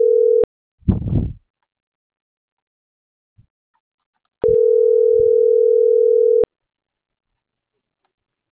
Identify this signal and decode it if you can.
Ringback tone (North America), 440 + 480 Hz.